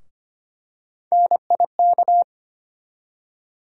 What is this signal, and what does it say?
Morse code, heard as a keyed tone, about 25 wpm, 700 Hz. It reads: NIK
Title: USA Call Sign Postfixes - 3 Letters 25wpm